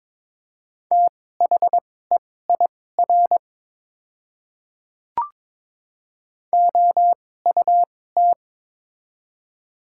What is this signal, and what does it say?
Morse code, heard as a keyed tone, about 22 wpm, spaced as for 16 wpm, 700 Hz. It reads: THEIR OUT
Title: 2x Character Spacing - Top 200 Words - 22wpm